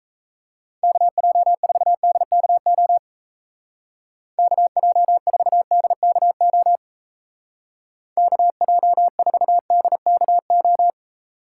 Morse code, heard as a keyed tone, about 34 wpm, 700 Hz. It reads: KJ4DKO KJ4DKO KJ4DKO
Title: Speed Racing - USA 2x3 Call Signs 28wpm